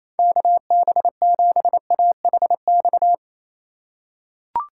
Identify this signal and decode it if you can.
Morse code, heard as a keyed tone, about 28 wpm, 700 Hz. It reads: KB7AHX
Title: USA Call Signs - 8x Character Spacing - 28wpm